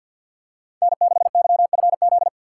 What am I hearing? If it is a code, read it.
Morse code, heard as a keyed tone, about 51 wpm, 700 Hz. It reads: N6YFZ
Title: USA Call Signs - 5 Characters - 2x Character Spacing - 50wpm